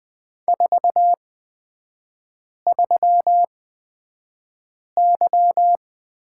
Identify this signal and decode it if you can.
Morse code, heard as a keyed tone, about 20 wpm, spaced as for 5 wpm, 690 Hz. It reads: 43Y
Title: 229 Lesson 37e - Letter x - Three Characters 20wpm